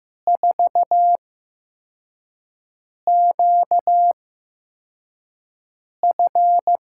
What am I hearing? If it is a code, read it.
Morse code, heard as a keyed tone, about 15 wpm, spaced as for 4 wpm, 700 Hz. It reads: 4QF
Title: USA Call Signs - 8x Character Spacing - 15wpm